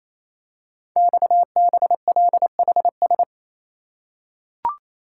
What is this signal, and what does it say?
Morse code, heard as a keyed tone, about 27 wpm, 700 Hz. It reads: XBLHS
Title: Code Groups - 5 Letters - 28wpm